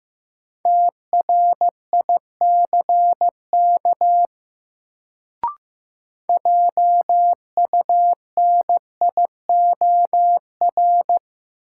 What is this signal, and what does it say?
Morse code, heard as a keyed tone, about 15 wpm, 700 Hz. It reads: TRICK JUNIOR
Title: Sentences from Top 1500 Words - New Word Drill Alt - 15wpm